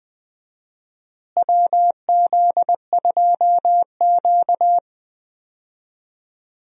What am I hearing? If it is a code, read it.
Morse code, heard as a keyed tone, about 21 wpm, 700 Hz. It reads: WZ2Q
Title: USA Call Signs - 4 Characters - Repeated 2x - 20wpm